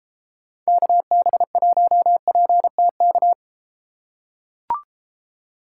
Morse code, heard as a keyed tone, about 31 wpm, 700 Hz. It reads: KB1PTK